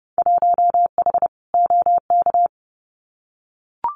Morse code, heard as a keyed tone, about 30 wpm, 700 Hz. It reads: 1H OK
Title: Winter Field Day Exchanges 30wpm